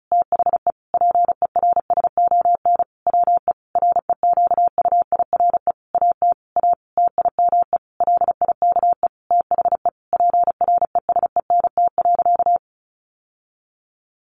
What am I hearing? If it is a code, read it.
Morse code, heard as a keyed tone, about 34 wpm, 700 Hz. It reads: THE PERSON WE REQUIRE AT A TIME LIKE THE PRESENT.
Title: Sentences from Top 900 Words 35wpm